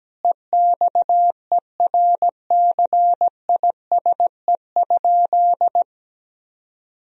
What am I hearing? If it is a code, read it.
Morse code, heard as a keyed tone, about 17 wpm, 700 Hz. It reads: EXERCISE?
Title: Sentences from Top 1000 Words - Review Part 4 - 17wpm